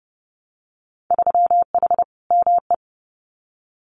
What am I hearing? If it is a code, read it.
Morse code, heard as a keyed tone, about 30 wpm, 700 Hz. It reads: 3H ME
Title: Winter Field Day Exchanges Encore 30wpm